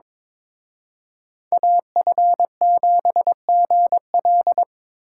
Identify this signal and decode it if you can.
Morse code, heard as a keyed tone, about 25 wpm, 700 Hz. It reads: AF7GL
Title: Speed Racing - USA 2x2 Call Signs 22wpm